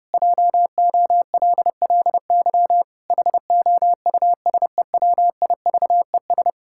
Morse code, heard as a keyed tone, about 30 wpm, 700 Hz. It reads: JOLLY HOUSEWIVES
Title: Pangram Challenge v2 - 30wpm